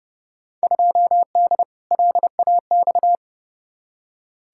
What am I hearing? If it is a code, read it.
Morse code, heard as a keyed tone, about 30 wpm, 700 Hz. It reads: 2D LAX
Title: Field Day Exchanges - 30wpm